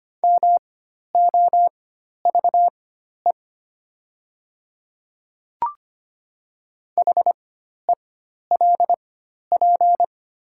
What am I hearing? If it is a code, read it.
Morse code, heard as a keyed tone, about 25 wpm, spaced as for 12 wpm, 700 Hz. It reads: MOVE HELP
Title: 4x Character Spacing - Top 300 Words - 25wpm